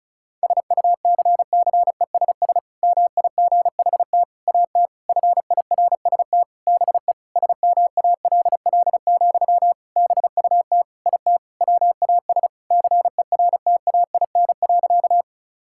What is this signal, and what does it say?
Morse code, heard as a keyed tone, about 36 wpm, 700 Hz. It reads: SUCCESS MIGHT AT FIRST BE SMALL, BUT IT WAS CERTAIN.
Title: Sentences from Top 1000 Words - Review Part 6 - 35wpm